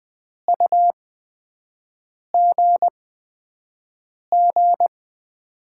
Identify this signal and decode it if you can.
Morse code, heard as a keyed tone, about 21 wpm, spaced as for 5 wpm, 700 Hz. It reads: UGG
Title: USA Call Signs - 6 Characters - 8x Character Spacing - 20wpm